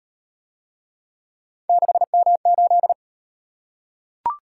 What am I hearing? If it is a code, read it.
Morse code, heard as a keyed tone, about 36 wpm, 700 Hz. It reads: 6M8